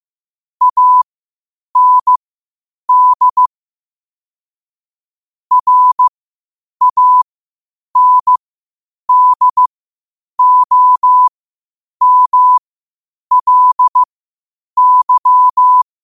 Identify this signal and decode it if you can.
Morse code, heard as a keyed tone, about 15 wpm, spaced as for 8 wpm, 1000 Hz. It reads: AND RANDOMLY
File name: qotdfiles / 8wpm